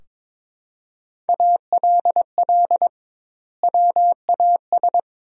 Morse code, heard as a keyed tone, about 21 wpm, 700 Hz. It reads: ALL WAS
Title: 2x Word-Spacing - Sentences from Top 500 Words - 22wpm